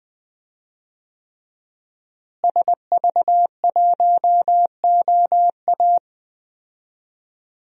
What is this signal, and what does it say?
Morse code, heard as a keyed tone, about 20 wpm, 700 Hz. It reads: SV1OA
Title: DXCC-100 Call Signs - 5 Characters - Repeated 3x - 20wpm